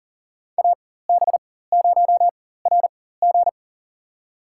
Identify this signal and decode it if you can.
Morse code, heard as a keyed tone, about 40 wpm, spaced as for 19 wpm, 700 Hz. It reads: AB0RG
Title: USA Call Signs - 5 Characters - 4x Character Spacing - 40wpm